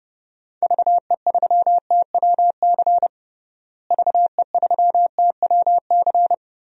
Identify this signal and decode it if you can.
Morse code, heard as a keyed tone, about 32 wpm, 700 Hz. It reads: VE3TWC VE3TWC
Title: DXCC-100 Call Signs - 6 Characters - Repeated 3x - 30wpm